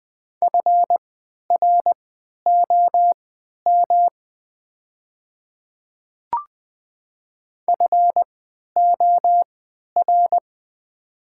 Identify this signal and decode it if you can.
Morse code, heard as a keyed tone, about 20 wpm, spaced as for 11 wpm, 700 Hz. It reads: FROM FOR
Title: 3x Character Spacing - Top 100 Words - 20wpm